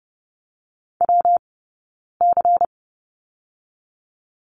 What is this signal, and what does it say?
Morse code, heard as a keyed tone, about 30 wpm, 700 Hz. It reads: W C
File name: Lesson 19d - Letter w - Two Characters 30wpm